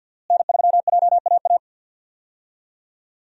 Morse code, heard as a keyed tone, about 48 wpm, 700 Hz. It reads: N3JAA
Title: USA Call Signs - 5 Characters - Repeated 2x - 50wpm